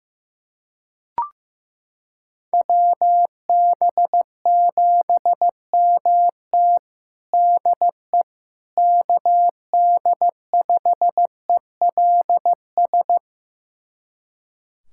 Morse code, heard as a keyed tone, about 15 wpm, 700 Hz. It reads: WB7MT DE KD5ELS
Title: USA Callsign DE Callsign 15wpm